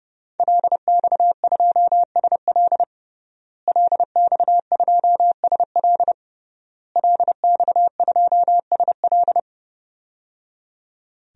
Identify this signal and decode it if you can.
Morse code, heard as a keyed tone, about 31 wpm, 700 Hz. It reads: LX2SL LX2SL LX2SL
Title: DXCC-100 Call Signs - 5 Characters - Repeated 3x - 30wpm